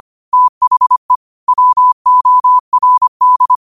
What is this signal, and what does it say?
Morse code, heard as a keyed tone, about 25 wpm, 1000 Hz. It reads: THE WORD
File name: qotdfiles / 25wpm